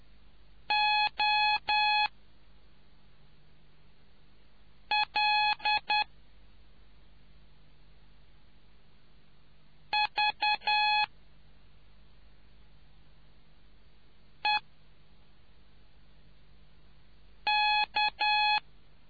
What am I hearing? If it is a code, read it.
Morse code, heard as a keyed tone, about 10 wpm, spaced as for 2 wpm, 840 Hz. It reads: OLVEK